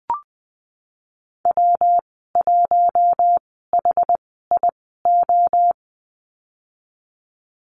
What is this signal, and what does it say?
Morse code, heard as a keyed tone, about 20 wpm, spaced as for 15 wpm, 700 Hz. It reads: W1HIO
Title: 147 Lesson 24g - Letter y - Call Signs 20wpm